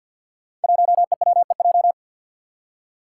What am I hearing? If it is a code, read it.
Morse code, heard as a keyed tone, about 51 wpm, 700 Hz. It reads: 1EWEJ
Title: Five Challenging Letter-Number Combinations 50wpm